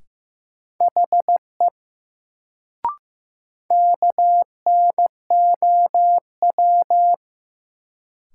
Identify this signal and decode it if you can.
Morse code, heard as a keyed tone, about 15 wpm, 700 Hz. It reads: HE KNOW
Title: Sentences from Top 100 Words - New Word Drill - 15wpm